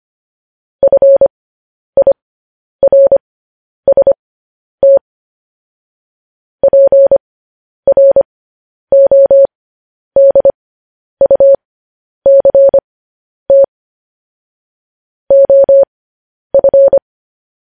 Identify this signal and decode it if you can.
Morse code, heard as a keyed tone, about 25 wpm, spaced as for 10 wpm, 550 Hz. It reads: FIRST PRODUCT OF